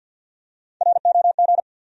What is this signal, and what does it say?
Morse code, heard as a keyed tone, about 52 wpm, 700 Hz. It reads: ROG
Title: USA Common Names - Max 4 Letters - 50wpm